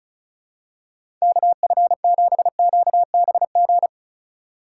Morse code, heard as a keyed tone, about 34 wpm, 700 Hz. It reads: KF7QBG
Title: USA Call Signs - 3x Character Spacing - 35wpm